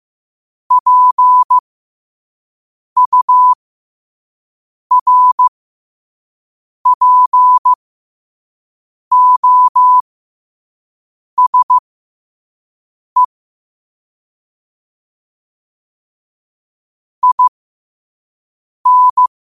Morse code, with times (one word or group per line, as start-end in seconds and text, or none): 0.7-13.2 PURPOSE
17.2-19.3 IN